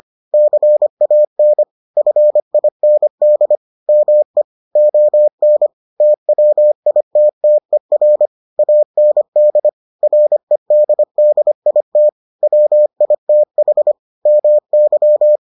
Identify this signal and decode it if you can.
Morse code, heard as a keyed tone, about 25 wpm, 600 Hz. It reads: CAN FIND ME ON TWITTER AND REDDIT WITH MY